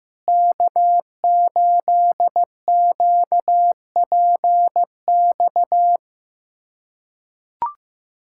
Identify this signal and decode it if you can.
Morse code, heard as a keyed tone, about 16 wpm, 700 Hz. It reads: K8QPX
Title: USA Callsign DE Callsign - 5 Characters - 15wpm